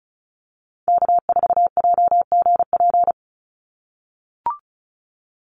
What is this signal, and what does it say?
Morse code, heard as a keyed tone, about 34 wpm, 700 Hz. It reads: K4JGP
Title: USA Call Signs 35wpm